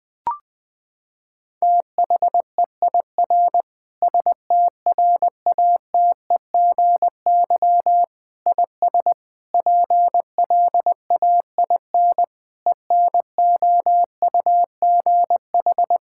Morse code, heard as a keyed tone, about 20 wpm, 700 Hz. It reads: THEIR STRATEGY IS PLAIN ENOUGH
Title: Sentences from Top 1000 Words - Review Part 5 - 20wpm